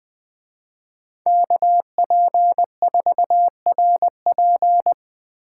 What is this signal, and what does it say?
Morse code, heard as a keyed tone, about 20 wpm, 700 Hz. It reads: KP4RP